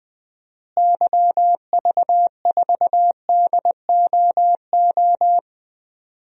Morse code, heard as a keyed tone, about 21 wpm, 700 Hz. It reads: YV4DOO
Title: DXCC-100 Call Signs - 6 Characters - Repeated 3x - 20wpm